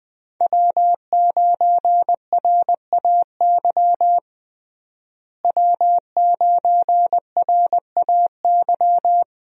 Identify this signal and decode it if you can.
Morse code, heard as a keyed tone, about 20 wpm, 700 Hz. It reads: W9RAY W9RAY
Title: USA Call Signs - 5 Characters - Repeated 2x - 20wpm